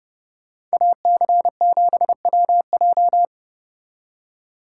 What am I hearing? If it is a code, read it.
Morse code, heard as a keyed tone, about 33 wpm, 700 Hz. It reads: AC7WJ